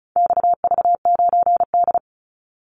Morse code, heard as a keyed tone, about 34 wpm, 700 Hz. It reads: XV9D
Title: Call Signs - 35wpm